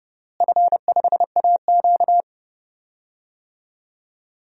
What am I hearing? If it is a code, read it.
Morse code, heard as a keyed tone, about 30 wpm, 700 Hz. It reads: F5AQ
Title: DXCC-100 Call Signs - 4 Characters - Repeated 2x - 30wpm